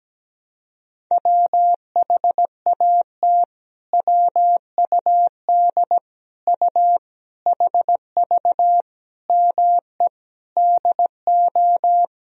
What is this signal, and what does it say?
Morse code, heard as a keyed tone, about 17 wpm, 700 Hz. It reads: WHAT WUD U HV ME DO
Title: Sentences from Top 300 Words and Abbreviations 17wpm